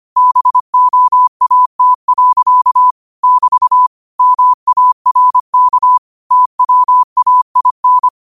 Morse code, heard as a keyed tone, about 25 wpm, 1000 Hz. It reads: BOAT. <BT> MARK TWAIN